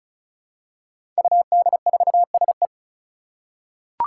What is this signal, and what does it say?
Morse code, heard as a keyed tone, about 34 wpm, 700 Hz. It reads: UD4SE
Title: Five Challenging Letter-Number Combinations 35wpm